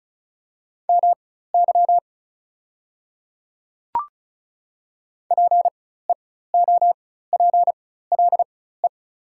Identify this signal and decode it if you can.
Morse code, heard as a keyed tone, about 36 wpm, spaced as for 16 wpm, 700 Hz. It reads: MY PEOPLE
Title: 4x Character Spacing - Top 100 Words - 35wpm